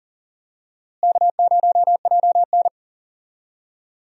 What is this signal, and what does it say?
Morse code, heard as a keyed tone, about 40 wpm, 700 Hz. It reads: K0JN
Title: USA Call Signs - 4 Characters - Repeated 3x - 40wpm